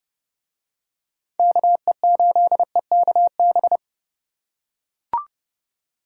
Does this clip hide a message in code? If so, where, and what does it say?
Yes: Morse code from 1.4 to 3.8 s, heard as a keyed tone, about 29 wpm, 700 Hz. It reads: KE8EKB